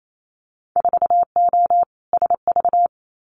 Morse code, heard as a keyed tone, about 28 wpm, 700 Hz. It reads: 4O SV
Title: Winter Field Day Exchanges Encore 28wpm